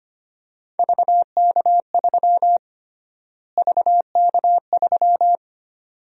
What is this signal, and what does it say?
Morse code, heard as a keyed tone, about 25 wpm, 700 Hz. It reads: VK3 VK3
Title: USA Call Sign Prefixes - Repeated 2x - 25wpm